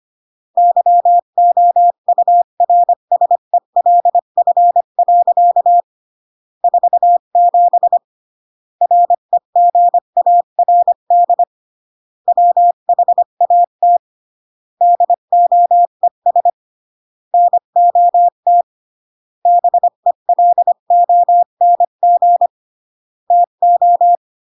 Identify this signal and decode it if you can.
Morse code, heard as a keyed tone, about 25 wpm, 700 Hz. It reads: YOURSELF. 47 REGARD WHAT DOES NOT BELONG TO